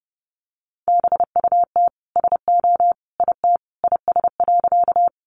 Morse code, heard as a keyed tone, about 30 wpm, 700 Hz. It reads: BUT SO IT IS.
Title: Sentences from Top 100 Words 30wpm